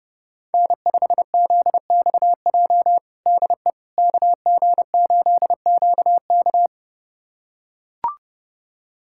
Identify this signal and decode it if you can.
Morse code, heard as a keyed tone, about 32 wpm, 700 Hz. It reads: N5ZXJ DE KG8QK
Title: USA Callsign DE Callsign - 5 Characters - 30wpm